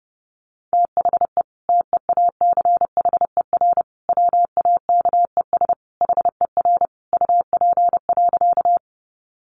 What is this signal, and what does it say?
Morse code, heard as a keyed tone, about 30 wpm, 700 Hz. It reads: THE TEACHER WAKES HER UP.
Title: Easy Phrases 30wpm